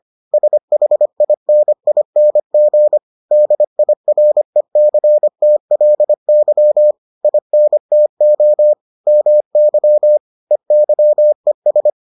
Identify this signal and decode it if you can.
Morse code, heard as a keyed tone, about 25 wpm, 600 Hz. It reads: SHINING DIRECTLY INTO MY EYES